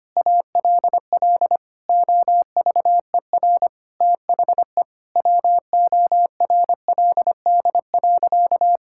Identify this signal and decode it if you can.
Morse code, heard as a keyed tone, about 25 wpm, 700 Hz. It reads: ALL OVER THE WORLD.
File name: practice / 25wpm / Sentences with 4-Word Phrases - 25wpm